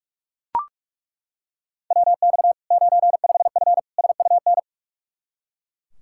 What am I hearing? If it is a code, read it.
Morse code, heard as a keyed tone, about 41 wpm, 700 Hz. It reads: WX 95F SUN
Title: Practice QSO - Exchange 2 - Info - 45wpm